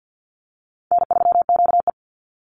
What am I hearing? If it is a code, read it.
Morse code, heard as a keyed tone, about 50 wpm, 700 Hz. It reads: N3QE
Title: Easy Call Signs Repeated 3x - 50wpm